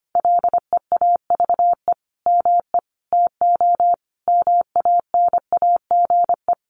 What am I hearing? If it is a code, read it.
Morse code, heard as a keyed tone, about 25 wpm, 700 Hz. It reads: LEAVE ME TO MANAGE
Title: Sentences from Top 1400 Words 25wpm